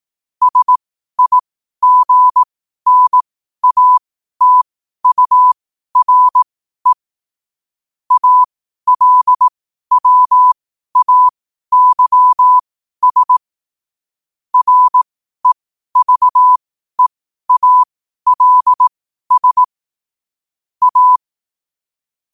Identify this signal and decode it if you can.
Morse code, heard as a keyed tone, about 18 wpm, spaced as for 12 wpm, 1000 Hz. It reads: SIGNATURE ALWAYS REVEALS A
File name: qotdfiles / 12wpm